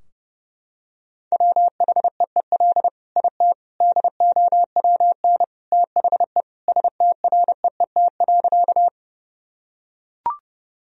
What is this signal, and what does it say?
Morse code, heard as a keyed tone, about 32 wpm, 700 Hz. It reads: WHEEL IT DOWN THE STREET.
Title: Sentences from Top 1000 Words - Review Part 3 - 30wpm